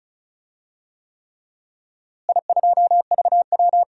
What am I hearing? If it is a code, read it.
Morse code, heard as a keyed tone, about 34 wpm, 700 Hz. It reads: I2VW
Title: DXCC-100 Call Signs - 4 Characters - Repeated 3x - 35wpm